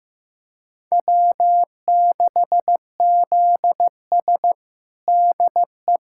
Morse code, heard as a keyed tone, about 16 wpm, 700 Hz. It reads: W6ZS DE
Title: USA Callsign DE Callsign - 4 Characters - 15wpm